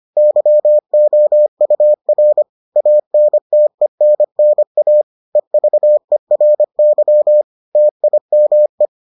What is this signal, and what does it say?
Morse code, heard as a keyed tone, about 25 wpm, 600 Hz. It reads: YOUR ANTENNA EVERY TIME